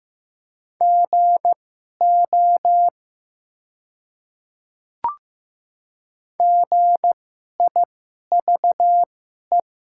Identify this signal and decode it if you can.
Morse code, heard as a keyed tone, about 15 wpm, spaced as for 11 wpm, 700 Hz. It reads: GO GIVE
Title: 2x Character Spacing - Top 300 Words - 15wpm